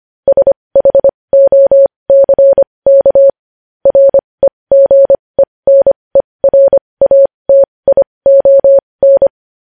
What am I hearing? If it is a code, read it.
Morse code, heard as a keyed tone, about 25 wpm, spaced as for 20 wpm, 550 Hz. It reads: SHOCK REGENERATION
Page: news-headlines-in-morse-code-at-20-wpm